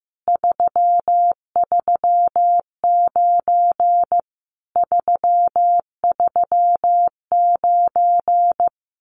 Morse code, heard as a keyed tone, about 16 wpm, 700 Hz. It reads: 339 339
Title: Common QSO Phrases Encore 15wpm